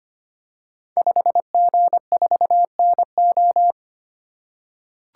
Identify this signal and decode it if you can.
Morse code, heard as a keyed tone, about 25 wpm, 700 Hz. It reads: 5G4NO
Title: Code Groups - 5 Letter:Numbers - 25wpm